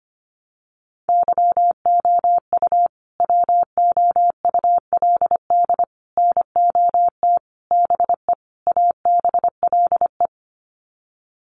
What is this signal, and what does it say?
Morse code, heard as a keyed tone, about 25 wpm, 700 Hz. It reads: YOU WOULD NOT BE ABLE
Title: Sets of 5 Words Encore2 25wpm